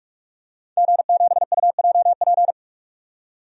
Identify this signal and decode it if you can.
Morse code, heard as a keyed tone, about 46 wpm, 700 Hz. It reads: G7UJP
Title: DXCC-100 Call Signs - 5 Characters - 45wpm